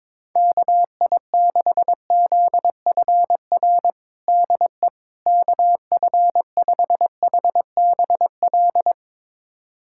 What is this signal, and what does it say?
Morse code, heard as a keyed tone, about 22 wpm, 700 Hz. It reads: KI6ZFR DE KF5HBL
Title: USA Callsign DE Callsign - 6 Characters - 22wpm